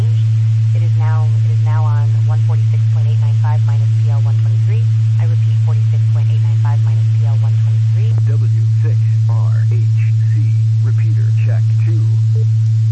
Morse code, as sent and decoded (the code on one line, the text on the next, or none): none